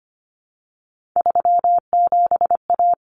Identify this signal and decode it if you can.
Morse code, heard as a keyed tone, about 25 wpm, 700 Hz. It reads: 37A
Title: Common Cut Numbers - 3-Digit 25wpm